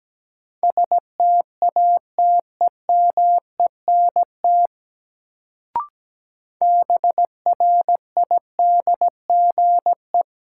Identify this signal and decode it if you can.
Morse code, heard as a keyed tone, about 17 wpm, 700 Hz. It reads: STATEMENT BRIDGE
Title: Sentences from Top 1200 Words - New Word Drill Alt - 17wpm